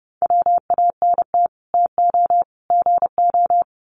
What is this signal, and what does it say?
Morse code, heard as a keyed tone, about 30 wpm, 700 Hz. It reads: WANT TO GO
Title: Sets of 3 Words 30wpm